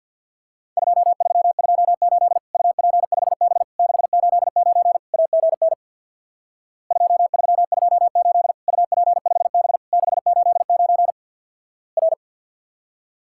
Morse code, heard as a keyed tone, about 50 wpm, 700 Hz. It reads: 2328 UP5B 689 AGN 2328 UP5B 689 R
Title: Worked All Europe DX Contest - QTC Exchange with Repeats - 50wpm